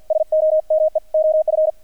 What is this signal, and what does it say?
Morse code, heard as a keyed tone, about 48 wpm, 620 Hz, with radio noise. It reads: SOME OF
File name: archived / CRSnet